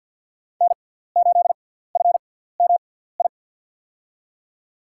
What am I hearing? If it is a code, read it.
Morse code, heard as a keyed tone, about 47 wpm, spaced as for 17 wpm, 700 Hz. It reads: N8FMI
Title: USA Call Signs - 5 Characters - 6x Character Spacing - 50wpm